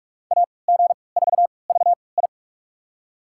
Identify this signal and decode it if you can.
Morse code, heard as a keyed tone, about 42 wpm, spaced as for 25 wpm, 700 Hz. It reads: AG4VI
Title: USA Call Signs - 5 Characters - 3x Character Spacing - 45wpm